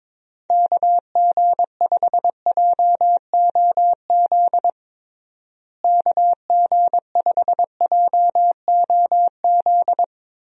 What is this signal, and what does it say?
Morse code, heard as a keyed tone, about 22 wpm, 700 Hz. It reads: KG5JOZ KG5JOZ
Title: USA Call Signs - 6 Characters - Repeated 2x - 22wpm